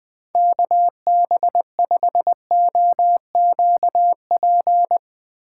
Morse code, heard as a keyed tone, about 20 wpm, 700 Hz. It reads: KB5OQP
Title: USA Call Signs - 6 Characters - 4x Character Spacing - 20wpm